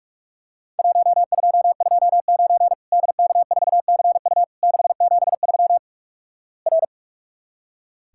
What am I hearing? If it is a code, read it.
Morse code, heard as a keyed tone, about 46 wpm, 700 Hz. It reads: 1229 DK4CU 673 R
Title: Worked All Europe DX Contest - QTC Exchange - 45wpm